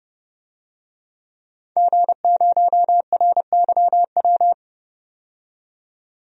Morse code, heard as a keyed tone, about 30 wpm, 700 Hz. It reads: G0RYW
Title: DXCC-100 Call Signs - 5 Characters - 30wpm